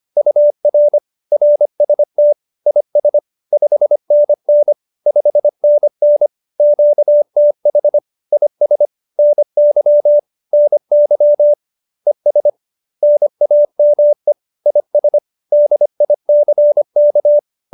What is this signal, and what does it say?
Morse code, heard as a keyed tone, about 25 wpm, 590 Hz. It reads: UR RST IS 5NN 5NN QTH IS NY NY ES NAME IS DICK
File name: QSO302 25